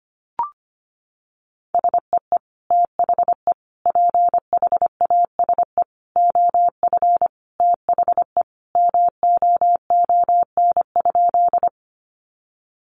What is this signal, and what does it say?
Morse code, heard as a keyed tone, about 25 wpm, 700 Hz. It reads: SEE THE PHASE OF THE MOON?
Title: Sentences from Top 1200 Words 25wpm